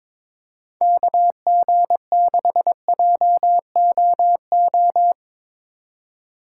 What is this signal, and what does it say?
Morse code, heard as a keyed tone, about 24 wpm, 700 Hz. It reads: KG6JOO
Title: Speed Racing - USA 2x3 Call Signs 22wpm